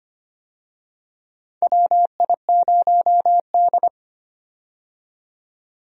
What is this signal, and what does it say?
Morse code, heard as a keyed tone, about 25 wpm, 700 Hz. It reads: WI0D
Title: USA Call Signs - 4 Characters - 25wpm